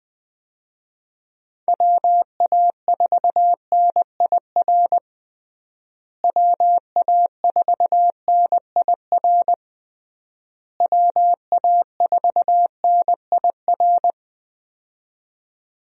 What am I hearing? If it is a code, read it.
Morse code, heard as a keyed tone, about 20 wpm, 700 Hz. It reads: WA4NIR WA4NIR WA4NIR
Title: USA Call Signs - 6 Characters - Repeated 3x - 20wpm